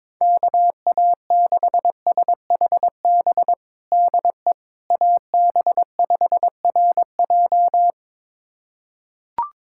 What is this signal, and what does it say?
Morse code, heard as a keyed tone, about 22 wpm, 700 Hz. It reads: KA6SHB DE AB5RJ
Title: USA Callsign DE Callsign 22wpm